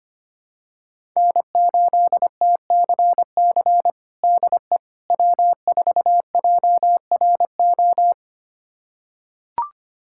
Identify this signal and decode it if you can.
Morse code, heard as a keyed tone, about 25 wpm, 700 Hz. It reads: N8TCC DE W4JRO
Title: USA Callsign DE Callsign - 5 Characters - 25wpm